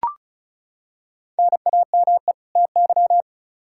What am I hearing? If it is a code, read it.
Morse code, heard as a keyed tone, about 34 wpm, 700 Hz. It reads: NAME TY
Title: Practice QSO - Exchange 1 - Beginning - 35wpm